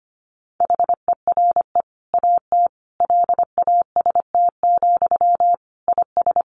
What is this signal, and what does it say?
Morse code, heard as a keyed tone, about 25 wpm, 700 Hz. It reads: HERE AT LAST, IS
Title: Sentences from Top 1100 Words 25wpm